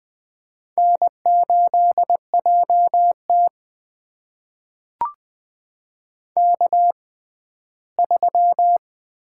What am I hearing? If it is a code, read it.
Morse code, heard as a keyed tone, about 20 wpm, spaced as for 14 wpm, 700 Hz. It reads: N8JT K3
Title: USA Call Signs - 4 Characters - 6x Character Spacing - 20wpm